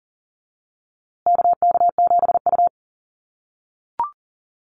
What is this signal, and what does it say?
Morse code, heard as a keyed tone, about 40 wpm, 700 Hz. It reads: KK7U
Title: USA Call Signs Repeated 2x 40wpm